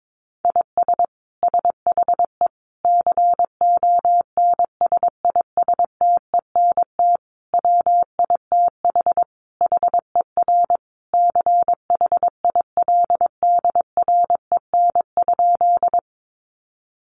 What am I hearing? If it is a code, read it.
Morse code, heard as a keyed tone, about 22 wpm, 700 Hz. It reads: IS SHE CONSISTENT WITH HER CHILDREN?
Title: Sentences from Top 1500 Words 22wpm